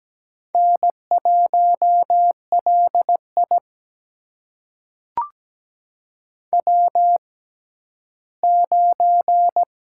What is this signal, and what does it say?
Morse code, heard as a keyed tone, about 17 wpm, spaced as for 11 wpm, 700 Hz. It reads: N1LI W9
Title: USA Call Signs - 4 Characters - 6x Character Spacing - 17wpm